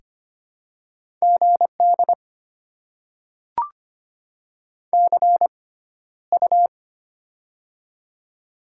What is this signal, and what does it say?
Morse code, heard as a keyed tone, about 25 wpm, spaced as for 13 wpm, 700 Hz. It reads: GD CU
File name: Top 90 QSO Elements - 6x Character Spacing - 25wpm